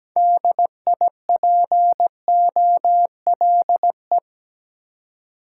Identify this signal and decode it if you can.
Morse code, heard as a keyed tone, about 20 wpm, 700 Hz. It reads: DIPOLE
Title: Speed-Racing Top 180 QSO Elements 17wpm